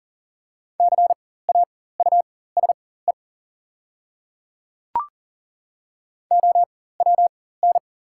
Morse code, heard as a keyed tone, about 38 wpm, spaced as for 18 wpm, 700 Hz. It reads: CAUSE OWN
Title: 4x Character Spacing - Top 300 Words - 40wpm